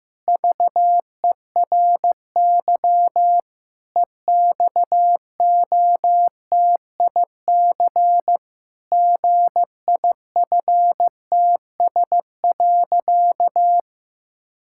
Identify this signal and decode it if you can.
Morse code, heard as a keyed tone, about 15 wpm, 700 Hz. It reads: VERY EXOTIC GIFTS.
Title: Pangram Challenge v2 - 15wpm